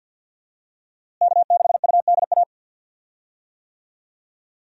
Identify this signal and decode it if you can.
Morse code, heard as a keyed tone, about 51 wpm, 700 Hz. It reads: K6UDA
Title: USA Call Signs - 5 Characters - Repeated 2x - 50wpm